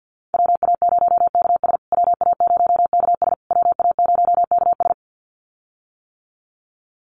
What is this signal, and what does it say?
Morse code, heard as a keyed tone, about 50 wpm, 700 Hz. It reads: WA0KS WA0KS WA0KS
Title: USA Call Signs Repeated 3x 50wpm